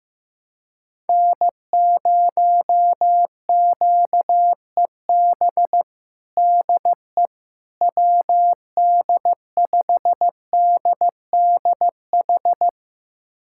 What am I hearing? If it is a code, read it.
Morse code, heard as a keyed tone, about 15 wpm, 700 Hz. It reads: N0QEB DE WD5DDH
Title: USA Callsign DE Callsign 15wpm